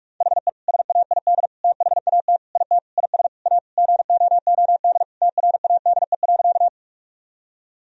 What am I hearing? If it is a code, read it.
Morse code, heard as a keyed tone, about 46 wpm, 700 Hz. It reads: HE SAID THAT IT IS A GOOD TRADE.
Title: Sentences from Top 1000 Words - Review Part 4 - 45wpm